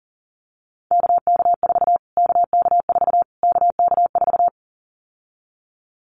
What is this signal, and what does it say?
Morse code, heard as a keyed tone, about 37 wpm, 700 Hz. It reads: KK4 KK4 KK4
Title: Callsign Prefixes for Top 25 DXCC - Repeated 3x - 40wpm